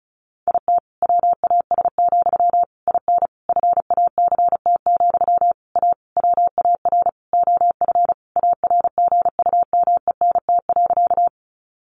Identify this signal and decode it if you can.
Morse code, heard as a keyed tone, about 34 wpm, 700 Hz. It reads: IT WAS, IN FACT, A WAR OF ARGUMENT.
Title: Sentences from Top 1400 Words 35wpm